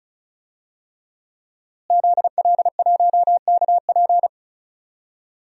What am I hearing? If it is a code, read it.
Morse code, heard as a keyed tone, about 36 wpm, 700 Hz. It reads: ZL1KP
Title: DXCC-100 Call Signs - 5 Characters - Repeated 3x - 35wpm